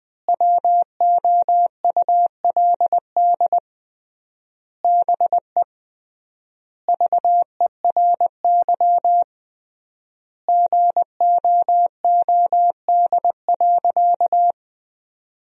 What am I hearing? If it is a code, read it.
Morse code, heard as a keyed tone, about 20 wpm, 700 Hz. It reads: WOULD BE VERY GOOD.
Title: 3x Word-Spacing - Sentences from Top 200 Words - 20wpm